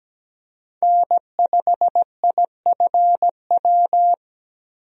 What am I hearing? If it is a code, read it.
Morse code, heard as a keyed tone, about 17 wpm, 700 Hz. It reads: N5IFW
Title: USA Call Signs - 5 Characters - Repeated 2x - 17wpm